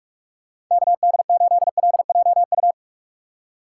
Morse code, heard as a keyed tone, about 46 wpm, 690 Hz. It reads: KD8LJU
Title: USA Call Signs - 6 Characters - 3x Character Spacing - 45wpm